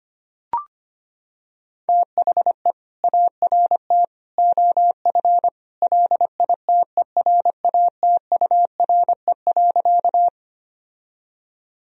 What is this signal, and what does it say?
Morse code, heard as a keyed tone, about 25 wpm, 700 Hz. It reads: THE ART OF LITERATURE.
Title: Sentences from Top 1000 Words - Review Part 3 - 25wpm